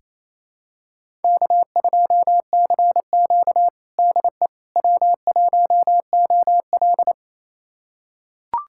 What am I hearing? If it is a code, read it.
Morse code, heard as a keyed tone, about 29 wpm, 690 Hz. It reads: K2CQ DE W1OL
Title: USA Callsign DE Callsign - 4 Characters - 28wpm